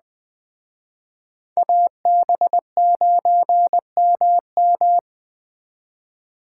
Morse code, heard as a keyed tone, about 19 wpm, 700 Hz. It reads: AB9MM